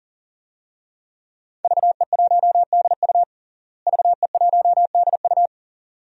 Morse code, heard as a keyed tone, about 40 wpm, 700 Hz. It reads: VE1DU VE1DU
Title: DXCC-100 Call Signs - 5 Characters - Repeated 2x - 40wpm